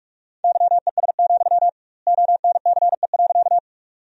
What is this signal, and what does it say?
Morse code, heard as a keyed tone, about 47 wpm, 700 Hz. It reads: YES, ONCE.
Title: 2x Word-Spacing - Sentences from Top 500 Words - 45wpm